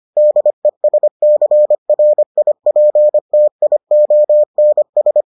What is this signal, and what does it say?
Morse code, heard as a keyed tone, about 25 wpm, 600 Hz. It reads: DESCRIPTIONS